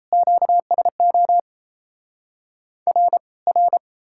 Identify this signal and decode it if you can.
Morse code, heard as a keyed tone, about 37 wpm, 700 Hz. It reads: QSO R R